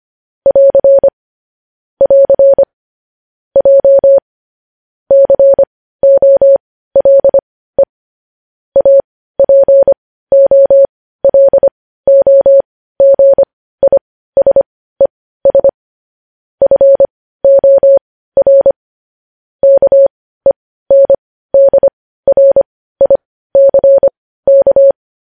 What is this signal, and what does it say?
Morse code, heard as a keyed tone, about 25 wpm, spaced as for 15 wpm, 550 Hz. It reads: <AR> <AR> J COLE APOLOGISES FOR KENDRICK
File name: podcast / News headlines-1x-25x15